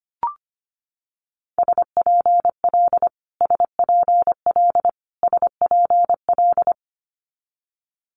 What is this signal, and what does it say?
Morse code, heard as a keyed tone, about 25 wpm, 700 Hz. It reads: SPL SPL SPL
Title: Multi-letter Phonemes Repeated 3x 25wpm